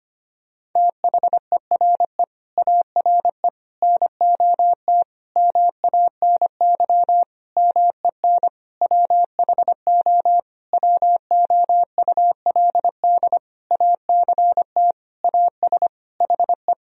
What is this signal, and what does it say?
Morse code, heard as a keyed tone, about 25 wpm, 700 Hz. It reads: THERE ARE NOT MANY MEN WHO WOULD ACT AS HE
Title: Sentences from Top 1000 Words - Review Part 1 - 25wpm